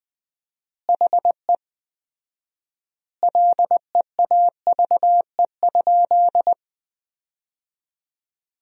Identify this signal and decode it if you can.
Morse code, heard as a keyed tone, about 21 wpm, 700 Hz. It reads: HE LEAVE?
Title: 4x Word-Spacing - Sentences from Top 400 Words - 20wpm